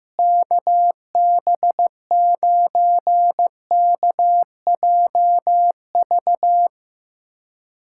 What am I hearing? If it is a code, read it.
Morse code, heard as a keyed tone, about 15 wpm, 700 Hz. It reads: KB9KJV
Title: USA Callsign DE Callsign - 6 Characters - 15wpm